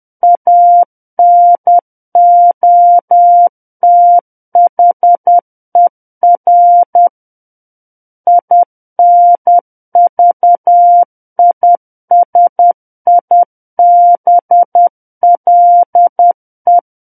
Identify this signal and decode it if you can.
Morse code, heard as a keyed tone, about 10 wpm, 700 Hz. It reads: ANOTHER INVISIBLE
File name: Book One / 01-Chapter 1 - 10 WPM